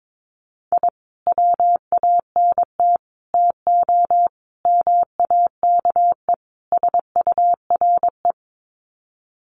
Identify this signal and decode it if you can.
Morse code, heard as a keyed tone, about 22 wpm, 700 Hz. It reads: I WANT TO MAKE SURE
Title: Sets of 5 Words Encore2 22wpm